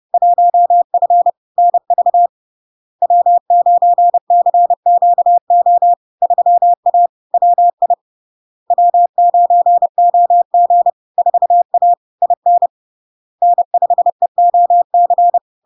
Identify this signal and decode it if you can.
Morse code, heard as a keyed tone, about 30 wpm, 700 Hz. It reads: 1F NV W9CQO 3A WI W9OG 4A IN N5EOC
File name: LARK-FD-2022-Log-in-CW-30wpm-700-Hz